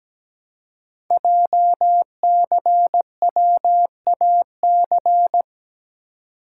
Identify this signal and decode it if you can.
Morse code, heard as a keyed tone, about 17 wpm, 700 Hz. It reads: JCWAC
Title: Code Groups - 5 Letters - 17wpm